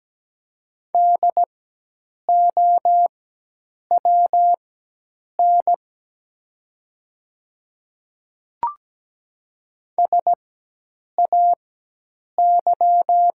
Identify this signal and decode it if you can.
Morse code, heard as a keyed tone, about 17 wpm, spaced as for 8 wpm, 700 Hz. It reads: DOWN SAY
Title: 4x Character Spacing - Top 300 Words - 17wpm